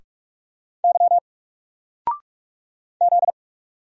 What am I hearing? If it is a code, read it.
Morse code, heard as a keyed tone, about 44 wpm, 700 Hz. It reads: Y Z
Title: Single Character (Letter:Number:Punctuation:Prosign) 45wpm